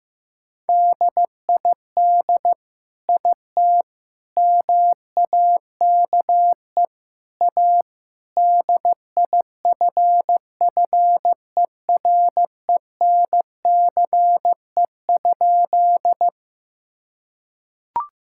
Morse code, heard as a keyed tone, about 16 wpm, 700 Hz. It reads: DID IT MAKE A DIFFERENCE?
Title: Sentences from Top 1000 Words - Review Part 5 - 15wpm